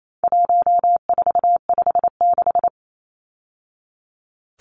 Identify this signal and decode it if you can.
Morse code, heard as a keyed tone, about 28 wpm, 700 Hz. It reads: 1456
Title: Numbers - 4-Digit 28wpm